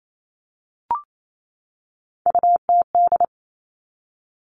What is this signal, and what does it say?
Morse code, heard as a keyed tone, about 28 wpm, 690 Hz. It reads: UTD
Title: Three Challenging Letter-Number Combinations 28wpm